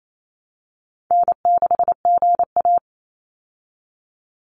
Morse code, heard as a keyed tone, about 28 wpm, 700 Hz. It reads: N6GA
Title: Easy Call Signs Repeated 2x - 28wpm